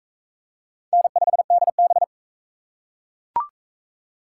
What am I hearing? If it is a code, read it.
Morse code, heard as a keyed tone, about 40 wpm, 700 Hz. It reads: N5DB